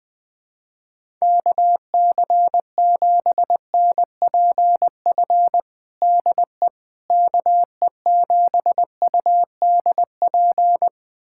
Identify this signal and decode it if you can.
Morse code, heard as a keyed tone, about 20 wpm, 700 Hz. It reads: KC7NPF DE KE7UDP
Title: USA Callsign DE Callsign - 6 Characters - 20wpm